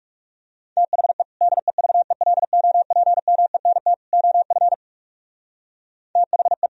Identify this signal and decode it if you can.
Morse code, heard as a keyed tone, about 43 wpm, 700 Hz. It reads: THE DEVELOPMENT OF THE